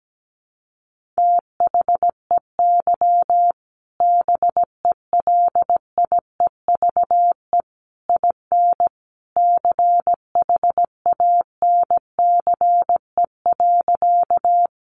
Morse code, heard as a keyed tone, about 17 wpm, 700 Hz. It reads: THEY BELIEVE IN CHANCE.
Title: Sentences from Top 1000 Words 17wpm